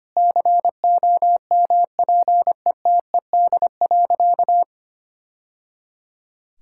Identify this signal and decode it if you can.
Morse code, heard as a keyed tone, about 25 wpm, 700 Hz. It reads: COMPETED.
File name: practice / 25wpm / Sentences with 3-Word Phrases - 25wpm